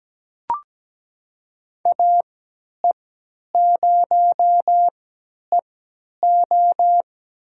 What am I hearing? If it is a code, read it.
Morse code, heard as a keyed tone, about 17 wpm, spaced as for 10 wpm, 700 Hz. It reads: AE0EO
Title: USA Call Signs - 5 Characters - 3x Character Spacing - 17wpm